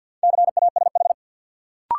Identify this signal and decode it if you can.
Morse code, heard as a keyed tone, about 52 wpm, 700 Hz. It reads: CASH
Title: Sentences from Top 700 Words - New Word Drill - 50wpm